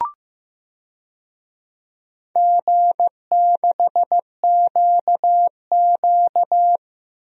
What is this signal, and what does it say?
Morse code, heard as a keyed tone, about 16 wpm, 700 Hz. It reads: G6QQ
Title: DXCC-100 Call Signs - 4 Characters - 15wpm